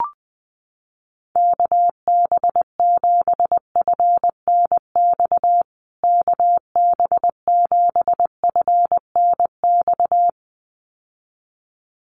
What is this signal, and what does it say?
Morse code, heard as a keyed tone, about 20 wpm, 700 Hz. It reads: KB7FNX KB7FNX
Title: USA Call Signs Repeated 2x 20wpm